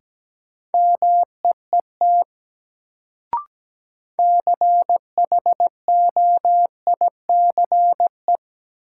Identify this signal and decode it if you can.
Morse code, heard as a keyed tone, about 17 wpm, 700 Hz. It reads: MEET CHOICE
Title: Sets of 5 Words - New Word Drill - 17wpm